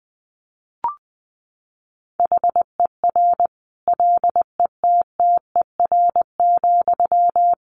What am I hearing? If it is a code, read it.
Morse code, heard as a keyed tone, about 20 wpm, 700 Hz. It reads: HER LETTER,
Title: Sentences from Top 1100 Words 20wpm